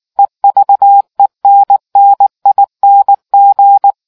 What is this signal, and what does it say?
Morse code, heard as a keyed tone, about 19 wpm, 800 Hz. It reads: EVENNING